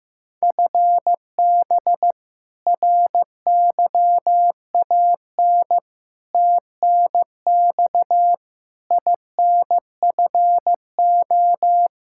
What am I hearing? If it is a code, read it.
Morse code, heard as a keyed tone, about 15 wpm, 700 Hz. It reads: FB RYAN TNX INFO
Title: K1USN SST QSO Practice 15wpm